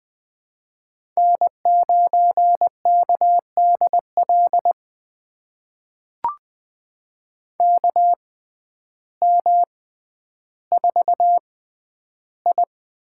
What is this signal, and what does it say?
Morse code, heard as a keyed tone, about 20 wpm, spaced as for 11 wpm, 700 Hz. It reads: N9KDL KM4I
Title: USA Call Signs - 5 Characters - 6x Character Spacing - 20wpm